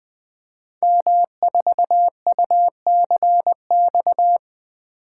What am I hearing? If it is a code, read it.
Morse code, heard as a keyed tone, about 20 wpm, 700 Hz. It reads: M4UCX
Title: Code Groups - 5 Letter:Numbers - 20wpm